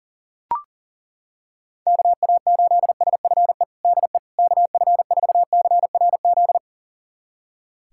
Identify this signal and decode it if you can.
Morse code, heard as a keyed tone, about 40 wpm, 700 Hz. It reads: KA8SFE DE KF4CRZ
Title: USA Callsign DE Callsign - 6 Characters - 40wpm